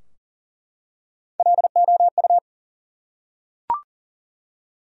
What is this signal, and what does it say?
Morse code, heard as a keyed tone, about 38 wpm, 700 Hz. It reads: LOU
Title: USA Common Names - Max 4 Letters - 40wpm